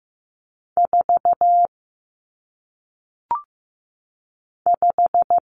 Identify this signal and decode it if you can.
Morse code, heard as a keyed tone, about 15 wpm, 700 Hz. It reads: 4 5
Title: Numbers - 1-Digit 15wpm